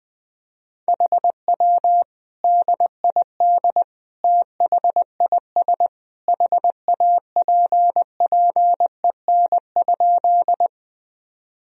Morse code, heard as a keyed tone, about 20 wpm, 700 Hz. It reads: HW DID THIS HAPPEN?
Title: Sentences from Top 400 Words and Abbreviations 20wpm